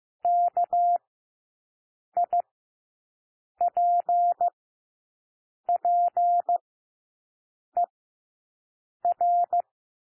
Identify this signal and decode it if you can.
Morse code, heard as a keyed tone, about 15 wpm, spaced as for 6 wpm, 700 Hz. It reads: KIPPER
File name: cwmp3 / W1 SOUND WORDS 15 wpm